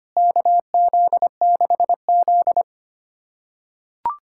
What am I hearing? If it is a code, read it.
Morse code, heard as a keyed tone, about 25 wpm, 700 Hz. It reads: KZ6Z
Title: USA Callsign DE Callsign - 4 Characters - 25wpm